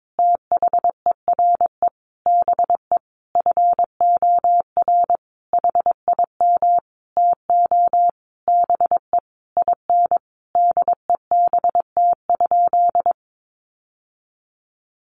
Morse code, heard as a keyed tone, about 22 wpm, 700 Hz. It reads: THERE BE FOR HIM TO BE IN DEBT?
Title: Sentences from Top 1300 Words 22wpm